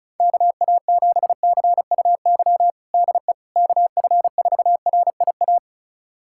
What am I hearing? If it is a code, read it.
Morse code, heard as a keyed tone, about 34 wpm, 700 Hz. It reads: KA7CUY DE KF4RIA
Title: USA Callsign DE Callsign 35wpm